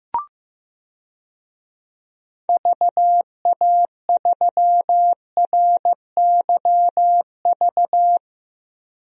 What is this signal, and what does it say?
Morse code, heard as a keyed tone, about 15 wpm, 700 Hz. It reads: VA3RYV
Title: DXCC-100 Call Signs - 6 Characters - Repeated 2x - 15wpm